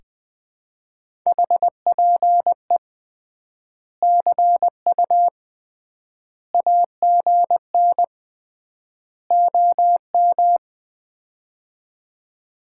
Morse code, heard as a keyed tone, about 21 wpm, 700 Hz. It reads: HPE CU AGN OM
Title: Common QSO Phrases - 3x Word-Spacing - 20wpm